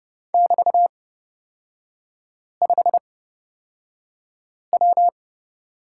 Morse code, heard as a keyed tone, about 31 wpm, spaced as for 5 wpm, 700 Hz. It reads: <BT>5W
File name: Lesson 36e - BT - Three Characters 30wpm